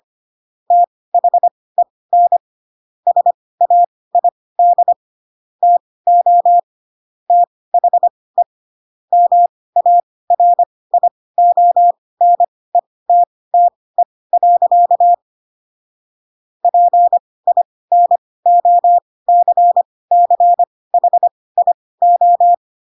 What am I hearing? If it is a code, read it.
Morse code, heard as a keyed tone, about 25 wpm, spaced as for 18 wpm, 700 Hz. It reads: THEN SAID TO THE MARIONETTE. PINOCCHIO